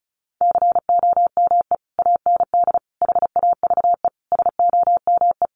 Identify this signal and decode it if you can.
Morse code, heard as a keyed tone, about 34 wpm, 700 Hz. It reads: COME AND HAVE SOME
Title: Sentences from Top 1500 Words 35wpm